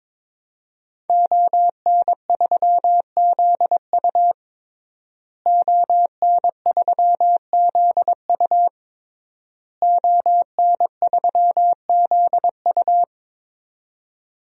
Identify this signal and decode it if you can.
Morse code, heard as a keyed tone, about 22 wpm, 700 Hz. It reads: ON3ZU ON3ZU ON3ZU
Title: DXCC-100 Call Signs - 5 Characters - Repeated 3x - 22wpm